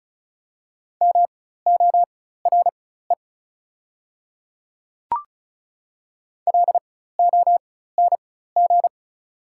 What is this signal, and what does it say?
Morse code, heard as a keyed tone, about 36 wpm, spaced as for 16 wpm, 700 Hz. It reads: MORE LONG
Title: 4x Character Spacing - Top 100 Words - 35wpm